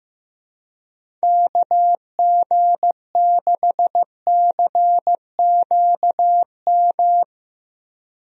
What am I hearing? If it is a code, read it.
Morse code, heard as a keyed tone, about 15 wpm, 700 Hz. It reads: KG6CQM
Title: USA Call Signs - 6 Characters - 15wpm